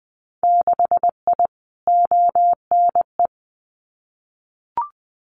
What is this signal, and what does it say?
Morse code, heard as a keyed tone, about 20 wpm, 700 Hz. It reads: 6I ONE
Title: Winter Field Day Exchanges Encore 20wpm